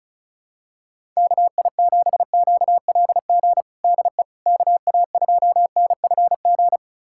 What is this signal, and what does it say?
Morse code, heard as a keyed tone, about 36 wpm, 690 Hz. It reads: KI7QLG DE KA2NFG
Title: USA Callsign DE Callsign - 6 Characters - 35wpm